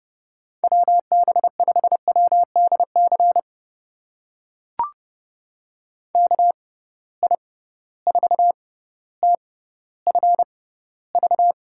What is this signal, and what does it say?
Morse code, heard as a keyed tone, about 29 wpm, spaced as for 15 wpm, 700 Hz. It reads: WB5WDC KI4TFV
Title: USA Call Signs - 6 Characters - 6x Character Spacing - 30wpm